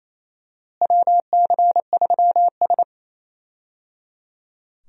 Morse code, heard as a keyed tone, about 28 wpm, 700 Hz. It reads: WC3S
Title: USA Call Signs - 4 Characters - Repeated 3x - 28wpm